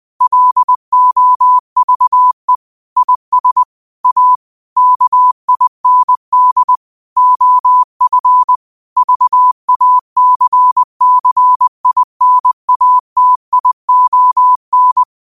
Morse code, heard as a keyed tone, about 20 wpm, 1000 Hz. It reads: LOVE IS A KIND OF VACCINATION